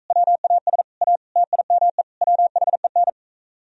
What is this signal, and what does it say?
Morse code, heard as a keyed tone, about 45 wpm, 700 Hz. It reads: WAS A TIME WHEN